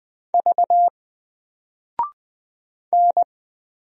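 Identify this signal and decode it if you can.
Morse code, heard as a keyed tone, about 20 wpm, 700 Hz. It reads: V N
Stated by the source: Single letter and number Morse drill at 20 wpm, from N4LCD.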